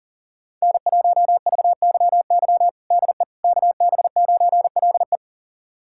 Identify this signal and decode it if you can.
Morse code, heard as a keyed tone, about 40 wpm, 700 Hz. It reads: N1VYY DE KB9LE
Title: USA Callsign DE Callsign - 5 Characters - 40wpm